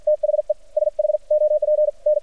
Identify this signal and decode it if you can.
Morse code, heard as a keyed tone, about 56 wpm, spaced as for 47 wpm, 600 Hz. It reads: THE SHOP N
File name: archived / CRSnet